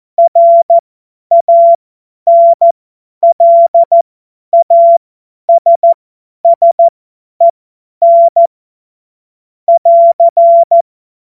Morse code, heard as a keyed tone, about 14 wpm, spaced as for 10 wpm, 680 Hz. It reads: RANLASSEN <AR>
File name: Morsekurs / Witze / Witz 06 Februar 2024 50 F70